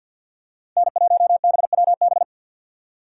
Morse code, heard as a keyed tone, about 53 wpm, 700 Hz. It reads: N1BWB
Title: USA Call Signs - 5 Characters - 8x Character Spacing - 50wpm